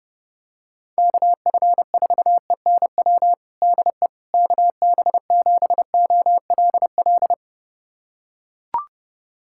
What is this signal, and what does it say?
Morse code, heard as a keyed tone, about 30 wpm, 700 Hz. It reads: KF4ENW DE KB7OLL
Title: USA Callsign DE Callsign - 6 Characters - 30wpm